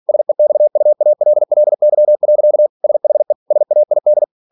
Morse code, heard as a keyed tone, about 46 wpm, 600 Hz. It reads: SEXUALLY. SHE SAID